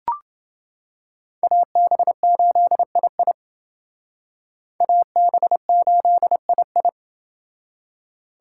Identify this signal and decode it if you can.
Morse code, heard as a keyed tone, about 28 wpm, 700 Hz. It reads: AB8II AB8II